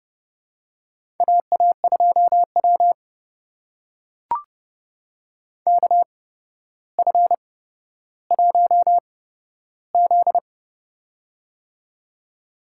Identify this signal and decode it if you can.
Morse code, heard as a keyed tone, about 30 wpm, spaced as for 13 wpm, 700 Hz. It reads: AA2W KF1Z
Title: USA Call Signs - 4 Characters - 8x Character Spacing - 30wpm